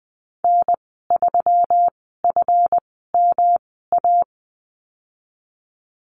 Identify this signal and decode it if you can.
Morse code, heard as a keyed tone, about 19 wpm, spaced as for 14 wpm, 700 Hz. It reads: N3FMA
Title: 147 Lesson 24g - Letter y - Call Signs 20wpm